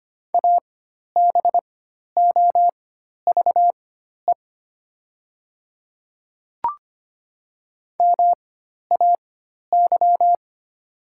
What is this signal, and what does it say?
Morse code, heard as a keyed tone, about 25 wpm, spaced as for 12 wpm, 700 Hz. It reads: ABOVE MAY
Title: 4x Character Spacing - Top 500 Words - 25wpm